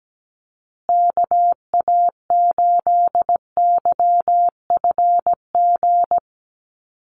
Morse code, heard as a keyed tone, about 17 wpm, 700 Hz. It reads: KA8YFG
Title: USA Call Signs Repeated 2x 17wpm